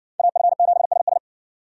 Morse code, heard as a keyed tone, about 60 wpm, 700 Hz. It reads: AF7SS